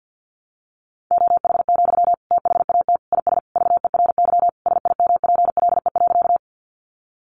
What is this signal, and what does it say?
Morse code, heard as a keyed tone, about 50 wpm, 700 Hz. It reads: OH, THAT IS VERY SIMPLE.
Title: Sentences from Top 500 Words 50wpm